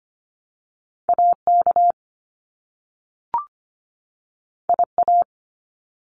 Morse code, heard as a keyed tone, about 25 wpm, 700 Hz. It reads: AK IA
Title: US States 25wpm